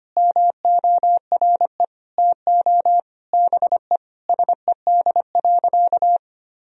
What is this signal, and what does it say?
Morse code, heard as a keyed tone, about 25 wpm, 700 Hz. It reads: MORE TO BE SED.
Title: Sentences from Top 500 Words and Abbreviations 25wpm